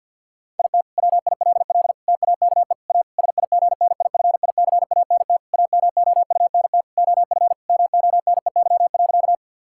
Morse code, heard as a keyed tone, about 50 wpm, 700 Hz. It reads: IT WILL TAKE A SIGNIFICANT AMOUNT OF MONEY.
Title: Sentences from Top 1000 Words - Review Part 6 - 50wpm